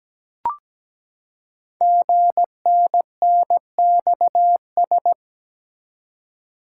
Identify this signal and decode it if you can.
Morse code, heard as a keyed tone, about 17 wpm, 700 Hz. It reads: GNNXS
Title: Code Groups - 5 Letters - 17wpm